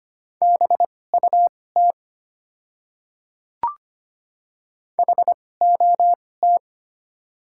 Morse code, heard as a keyed tone, about 25 wpm, spaced as for 18 wpm, 690 Hz. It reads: BUT HOT
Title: 2x Character Spacing - Top 100 Words - 25wpm